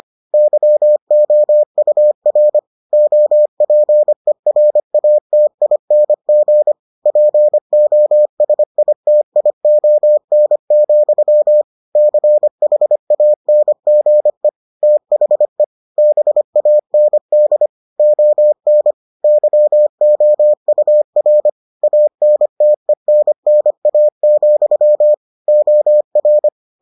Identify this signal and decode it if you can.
Morse code, heard as a keyed tone, about 25 wpm, 600 Hz. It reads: YOUR OPERATING POSITION, CHANGE THE BAND ON YOUR ANTENNA, OR